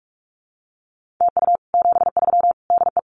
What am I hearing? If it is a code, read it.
Morse code, heard as a keyed tone, about 44 wpm, 700 Hz. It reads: TU 73 DE
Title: Common QSO Phrases Encore 45wpm